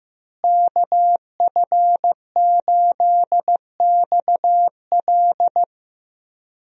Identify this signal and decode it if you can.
Morse code, heard as a keyed tone, about 15 wpm, 700 Hz. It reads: KF8XL
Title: USA Call Signs - 5 Characters - Repeated 2x - 15wpm